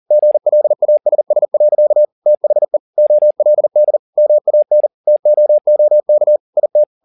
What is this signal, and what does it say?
Morse code, heard as a keyed tone, about 40 wpm, 600 Hz. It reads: GLASS. THE OLD MAN TOOK IT
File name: the old man and the sea 2 40wpm